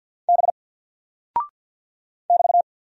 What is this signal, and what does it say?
Morse code, heard as a keyed tone, about 50 wpm, 700 Hz. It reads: B <BT>
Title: Single Character (Letter:Number:Punctuation:Prosign) 50wpm